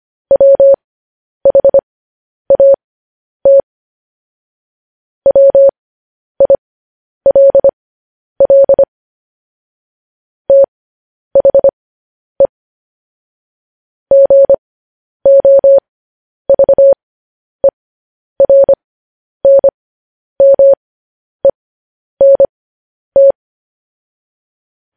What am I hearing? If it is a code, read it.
Morse code, heard as a keyed tone, about 25 wpm, spaced as for 10 wpm, 550 Hz. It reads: WHAT WILL THE GOVERNMENT
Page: news headlines in morse code at 10 wpm